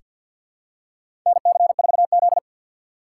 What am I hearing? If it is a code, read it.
Morse code, heard as a keyed tone, about 46 wpm, 700 Hz. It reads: NC4Z
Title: USA Call Signs - 4 Characters - 3x Character Spacing - 50wpm